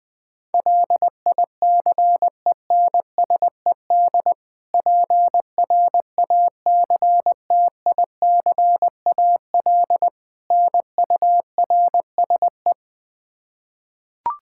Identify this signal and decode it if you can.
Morse code, heard as a keyed tone, about 20 wpm, 700 Hz. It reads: LICENSED PRACTICAL NURSE
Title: Practice QSO - Exchange 3 - More Info - 20wpm